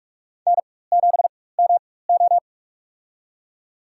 Morse code, heard as a keyed tone, about 46 wpm, spaced as for 21 wpm, 700 Hz. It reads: N7MO
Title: USA Call Signs - 4 Characters - 4x Character Spacing - 45wpm